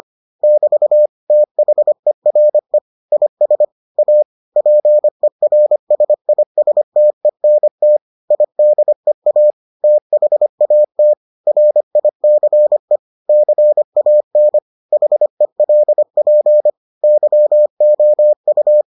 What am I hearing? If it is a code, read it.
Morse code, heard as a keyed tone, about 25 wpm, 600 Hz. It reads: <BT> THERE IS A PERSISTENT IDEA THAT RICE CAN HELP YOU